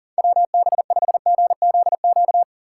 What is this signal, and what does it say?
Morse code, heard as a keyed tone, about 38 wpm, 700 Hz. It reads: WB5GZQ
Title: USA Call Signs - 6 Characters - 2x Character Spacing - 40wpm